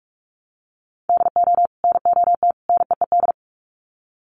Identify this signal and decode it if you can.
Morse code, heard as a keyed tone, about 44 wpm, 700 Hz. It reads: DO NOT NEED
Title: Sets of 3 Words 45wpm